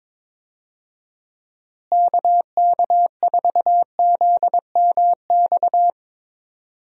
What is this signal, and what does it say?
Morse code, heard as a keyed tone, about 22 wpm, 700 Hz. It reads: KK4ZMX
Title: USA Call Signs - 6 Characters - Repeated 2x - 22wpm